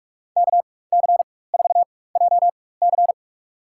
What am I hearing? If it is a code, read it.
Morse code, heard as a keyed tone, about 47 wpm, spaced as for 21 wpm, 690 Hz. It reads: KC4JC
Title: USA Call Signs - 5 Characters - 4x Character Spacing - 45wpm